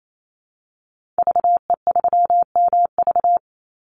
Morse code, heard as a keyed tone, about 28 wpm, 700 Hz. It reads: VE3MV
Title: Call Signs Repeated 3x - 28wpm